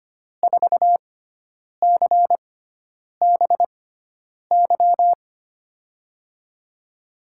Morse code, heard as a keyed tone, about 25 wpm, spaced as for 9 wpm, 700 Hz. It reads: 4CBY
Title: USA Call Signs - 6x Character Spacing - 25wpm